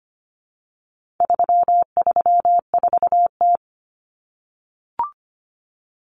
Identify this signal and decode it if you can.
Morse code, heard as a keyed tone, about 25 wpm, 700 Hz. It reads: 334T
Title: Common Cut Numbers - 4-Digit 25wpm